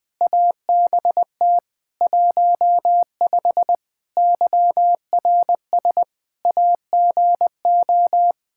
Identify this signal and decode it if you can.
Morse code, heard as a keyed tone, about 21 wpm, 700 Hz. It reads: ABT 15 YRS AGO
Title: Practice QSO - Exchange 3 - More Info - 20wpm